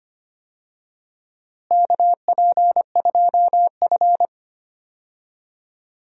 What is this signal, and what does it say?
Morse code, heard as a keyed tone, about 25 wpm, 700 Hz. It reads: KP2F
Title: DXCC-100 Call Signs - 4 Characters - Repeated 3x - 25wpm